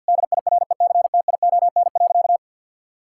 Morse code, heard as a keyed tone, about 51 wpm, 700 Hz. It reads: DIRECTION.
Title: Sentences from Top 1000 Words - Review Part 5 - 50wpm